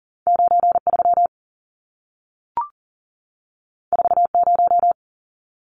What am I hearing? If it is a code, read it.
Morse code, heard as a keyed tone, about 40 wpm, 700 Hz. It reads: 93 40
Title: Numbers - 2-Digit 40wpm